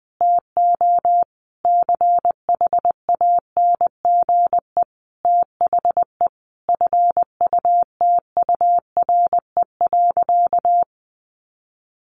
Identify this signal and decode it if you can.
Morse code, heard as a keyed tone, about 20 wpm, 700 Hz. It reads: TO CHANGE THE FUTURE.
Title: A Sound Of Thunder 20wpm